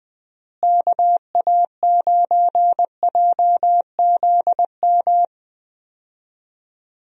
Morse code, heard as a keyed tone, about 20 wpm, 700 Hz. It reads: KA9JZM
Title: USA Call Signs - 6 Characters - Repeated 2x - 20wpm